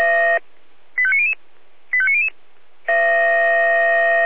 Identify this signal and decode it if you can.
Five-tone selective call (ZVEI), heard as quick runs of five tones, tones of about 70 ms. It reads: 85900 85900